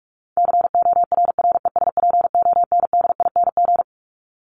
Morse code, heard as a keyed tone, about 44 wpm, 700 Hz. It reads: CORRESPONDING
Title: ING Suffix - Top 100 Words with Base Words 45wpm